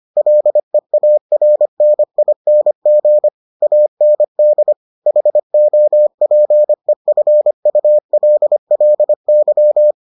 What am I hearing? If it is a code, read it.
Morse code, heard as a keyed tone, about 25 wpm, 600 Hz. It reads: LEARNING AND HOPEFULLY